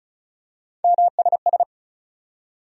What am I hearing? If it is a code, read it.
Morse code, heard as a keyed tone, about 36 wpm, 700 Hz. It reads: MSS
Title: USA Call Sign Postfixes - 3 Letters 35wpm